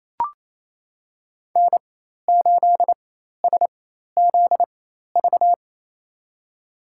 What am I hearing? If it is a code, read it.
Morse code, heard as a keyed tone, about 28 wpm, spaced as for 13 wpm, 700 Hz. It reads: N8SZV
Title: USA Call Signs - 5 Characters - 4x Character Spacing - 28wpm